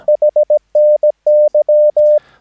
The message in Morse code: HNY